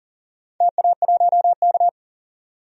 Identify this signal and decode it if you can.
Morse code, heard as a keyed tone, about 38 wpm, 700 Hz. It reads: TA1K
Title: DXCC-100 Call Signs - 4 Characters - 40wpm